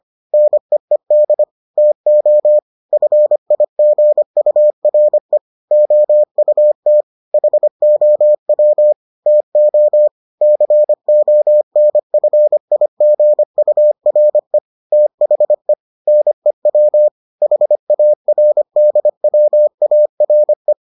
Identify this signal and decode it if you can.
Morse code, heard as a keyed tone, about 25 wpm, 600 Hz. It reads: NEED TO FIGURE OUT HOW TO CONFIGURE THE NEW HARDWARE